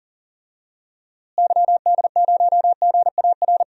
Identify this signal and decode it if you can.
Morse code, heard as a keyed tone, about 40 wpm, 700 Hz. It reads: YD0GAR
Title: DXCC-100 Call Signs - 6 Characters - Repeated 2x - 40wpm